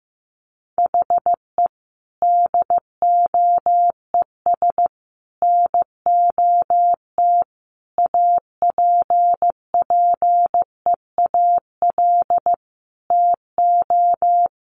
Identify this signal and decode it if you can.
Morse code, heard as a keyed tone, about 14 wpm, 700 Hz. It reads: HE DOES NOT APPEAL TO
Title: Sentences from Top 1300 Words 15wpm